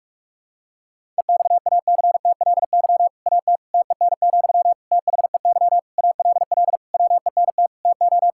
Morse code, heard as a keyed tone, about 46 wpm, 700 Hz. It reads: EXACTLY AT TEN, THEY ALL WENT TO
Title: Sentences from Top 1000 Words - Review Part 6 - 45wpm